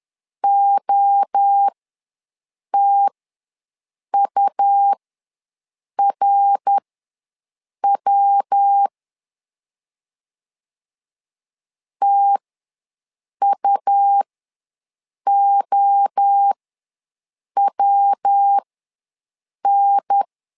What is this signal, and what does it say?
Morse code, heard as a keyed tone, about 10 wpm, spaced as for 6 wpm, 800 Hz. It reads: OTURW TUOWN
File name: LEZIONI / 3 MWRO / 3 MWRO ws 10wpm 2310